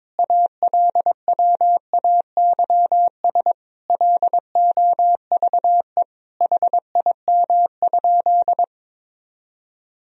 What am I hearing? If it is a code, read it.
Morse code, heard as a keyed tone, about 22 wpm, 700 Hz. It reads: ALWAYS LOVE HIM?
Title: Sentences from Top 400 Words and Abbreviations 22wpm